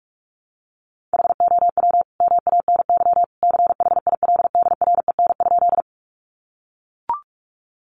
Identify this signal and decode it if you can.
Morse code, heard as a keyed tone, about 44 wpm, 700 Hz. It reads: HOW MANY CHILDREN?
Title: Sentences from Top 300 Words 45wpm